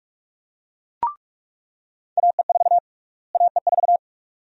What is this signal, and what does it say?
Morse code, heard as a keyed tone, about 42 wpm, 690 Hz. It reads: AE4 AE4
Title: USA Call Sign Prefixes - Repeated 2x - 45wpm